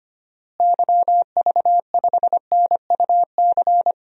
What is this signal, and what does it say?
Morse code, heard as a keyed tone, about 25 wpm, 700 Hz. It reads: YV5NUC
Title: DXCC-100 Call Signs - 6 Characters - Repeated 2x - 25wpm